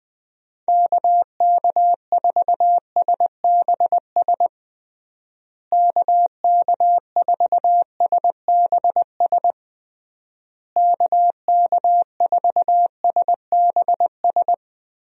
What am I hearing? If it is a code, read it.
Morse code, heard as a keyed tone, about 20 wpm, 700 Hz. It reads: KK4SBS KK4SBS KK4SBS
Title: USA Call Signs - 6 Characters - Repeated 3x - 20wpm